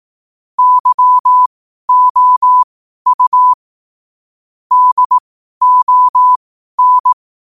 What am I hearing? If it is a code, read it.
Morse code, heard as a keyed tone, about 18 wpm, spaced as for 12 wpm, 1000 Hz. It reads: YOU DON
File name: qotdfiles / 12wpm